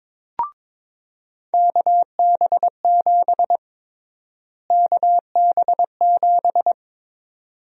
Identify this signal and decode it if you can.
Morse code, heard as a keyed tone, about 22 wpm, 700 Hz. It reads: KB7 KB7
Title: USA Call Sign Prefixes - Repeated 2x - 22wpm